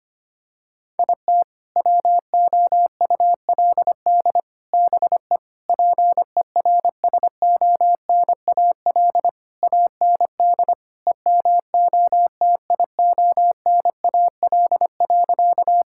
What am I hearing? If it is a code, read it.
Morse code, heard as a keyed tone, about 25 wpm, 700 Hz. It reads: IT WOULD BE PERSONAL AND EMOTIONAL.
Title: Sentences from Top 1000 Words - Review Part 5 - 25wpm